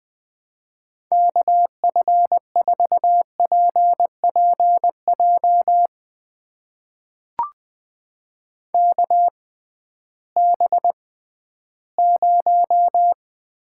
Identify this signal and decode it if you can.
Morse code, heard as a keyed tone, about 20 wpm, spaced as for 13 wpm, 700 Hz. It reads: KF4PPJ KB0
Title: USA Call Signs - 6 Characters - 6x Character Spacing - 20wpm